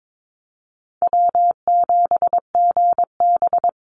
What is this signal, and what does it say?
Morse code, heard as a keyed tone, about 22 wpm, 700 Hz. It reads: W7GB
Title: Easy Call Signs Repeated 3x - 22wpm